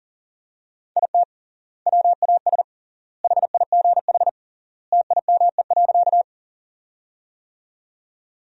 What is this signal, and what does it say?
Morse code, heard as a keyed tone, about 40 wpm, 700 Hz. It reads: IT WAS HIGH TIME.
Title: 3x Word-Spacing - Sentences from Top 200 Words - 40wpm